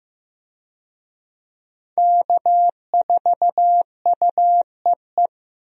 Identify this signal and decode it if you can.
Morse code, heard as a keyed tone, about 15 wpm, 700 Hz. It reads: K4UEE
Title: USA Call Signs - 5 Characters - Repeated 3x - 15wpm